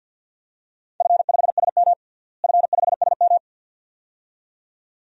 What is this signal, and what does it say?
Morse code, heard as a keyed tone, about 52 wpm, 700 Hz. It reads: F5SM F5SM
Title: DXCC-100 Call Signs - 4 Characters - Repeated 2x - 50wpm